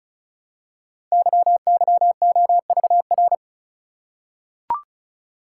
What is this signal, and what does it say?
Morse code, heard as a keyed tone, about 36 wpm, 700 Hz. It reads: YYOVR
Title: Code Groups - 5 Letters - 35wpm